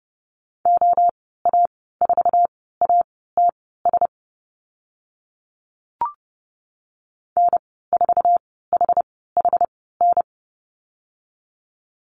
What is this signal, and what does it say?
Morse code, heard as a keyed tone, about 30 wpm, spaced as for 17 wpm, 700 Hz. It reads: OA4ATS N4HHN
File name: Lesson 14g - Number 2 - Call Signs 30wpm